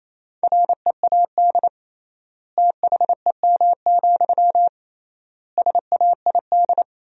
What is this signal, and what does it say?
Morse code, heard as a keyed tone, about 28 wpm, 700 Hz. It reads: READ THEM, SAID
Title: 3x Word-Spacing - Sentences from Top 400 Words - 28wpm